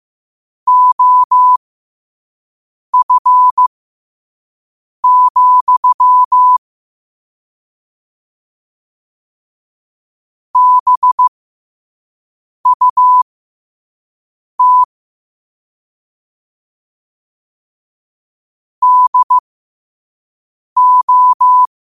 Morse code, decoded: OF, BUT DO

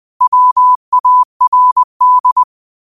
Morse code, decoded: WARD

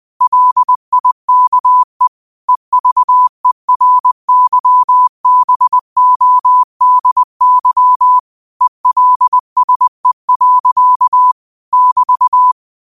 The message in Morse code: LIKE EVERYBODY ELSE. <BT>

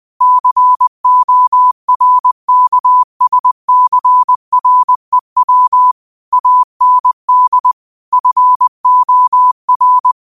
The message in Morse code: CORKSCREW AND FOR

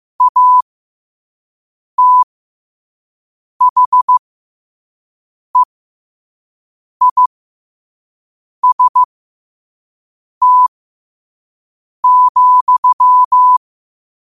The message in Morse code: ATHEIST,

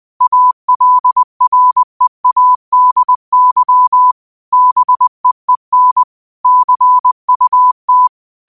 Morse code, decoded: ALREADY BEEN CUT

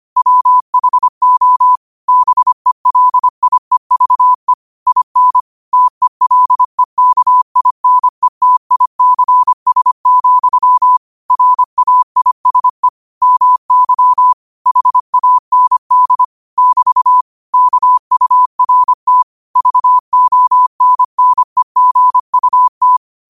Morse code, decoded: WHO BELIEVE IN TELEKINETICS, RAISE MY HAND <BT> KURT VONNEGUT

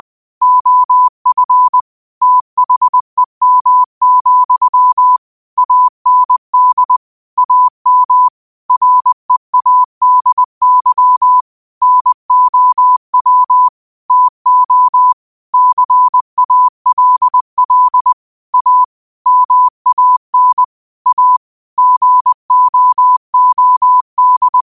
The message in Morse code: OF THEM, AND AM READY NOW TO CALL A MAN A GOOD

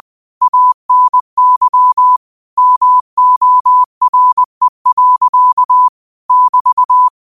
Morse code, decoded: ANY MORE. <BT>

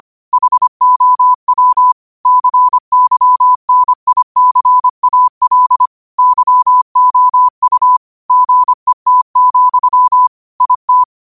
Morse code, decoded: HOW CYNICAL YOU GET, IT

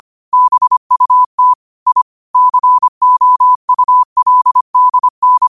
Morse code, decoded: BUT I COULDN

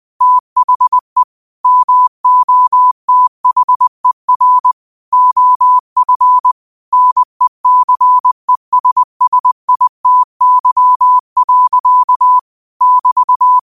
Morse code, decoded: THE MOTHER OF NECESSITY. <BT>

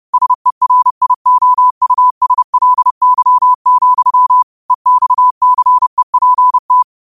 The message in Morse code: SERIOUSLY, EXCEPT

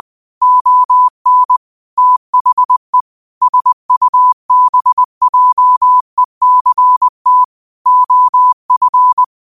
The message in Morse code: ON THE SUBJECT OF

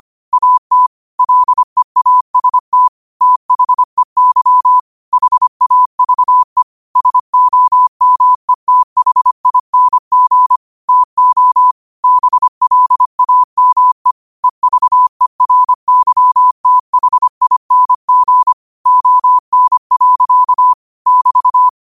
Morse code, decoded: AT LEAST THEY HAVE SOMETHING TO BLAME EVERYTHING ON. <BT>

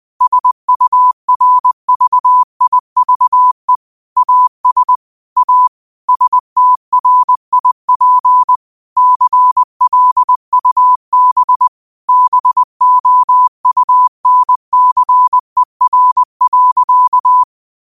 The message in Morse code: SURVIVE AS A STRIP CLUB BOUNCER.

